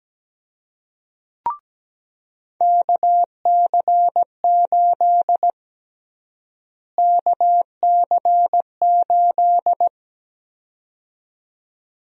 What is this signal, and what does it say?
Morse code, heard as a keyed tone, about 17 wpm, 700 Hz. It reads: KC8 KC8
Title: USA Call Sign Prefixes - Repeated 2x - 17wpm